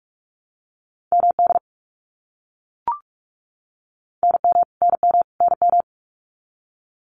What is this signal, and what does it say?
Morse code, heard as a keyed tone, about 44 wpm, 700 Hz. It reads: MD NM NM NM
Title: US States Repeated 3x 45wpm